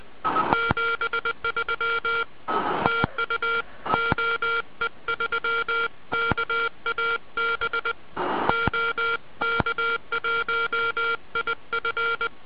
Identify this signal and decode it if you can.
Morse code, heard as a keyed tone, about 20 wpm, 1390 Hz. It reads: 73 TU OE3KAB OK1IF